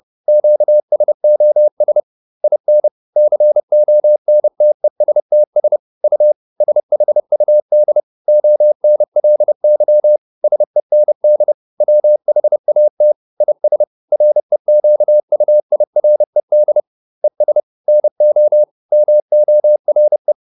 Morse code, heard as a keyed tone, about 30 wpm, 600 Hz. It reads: QSOS IN CONTESTS U SHUD ONLY SEND WHAT IS REQUIRED ES NO MORE